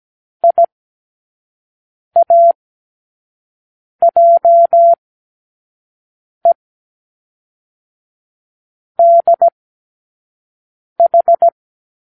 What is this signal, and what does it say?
Morse code, heard as a keyed tone, about 17 wpm, 680 Hz. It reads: I A J E D H
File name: k7qo / Lesson15(A-J)